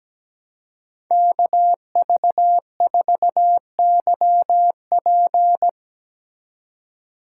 Morse code, heard as a keyed tone, about 18 wpm, 700 Hz. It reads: KV4YP
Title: Speed Racing - USA Call Signs 17wpm